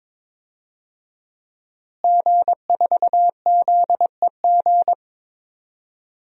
Morse code, heard as a keyed tone, about 22 wpm, 700 Hz. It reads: G4ZEG
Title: DXCC-100 Call Signs - 5 Characters - Repeated 3x - 22wpm